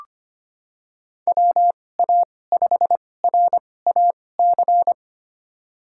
Morse code, heard as a keyed tone, about 25 wpm, spaced as for 18 wpm, 700 Hz. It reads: WA5RAC
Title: USA Call Signs - 6 Characters - 2x Character Spacing - 25wpm